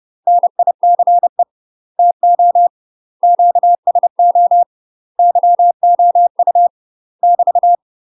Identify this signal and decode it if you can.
Morse code, heard as a keyed tone, about 30 wpm, 700 Hz. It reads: NICE TO QSO YOU <BT>
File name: UK MASTER QSOs TRAINING / 30WPM / UK MASTER QSOs 0045